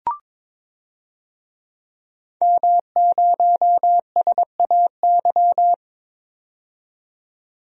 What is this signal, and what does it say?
Morse code, heard as a keyed tone, about 22 wpm, 700 Hz. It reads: M0SAY
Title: DXCC-100 Call Signs - 5 Characters - 22wpm